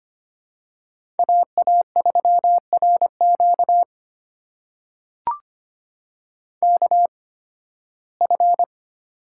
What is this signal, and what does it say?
Morse code, heard as a keyed tone, about 25 wpm, spaced as for 16 wpm, 700 Hz. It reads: AA3RQ KF
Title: USA Call Signs - 5 Characters - 8x Character Spacing - 25wpm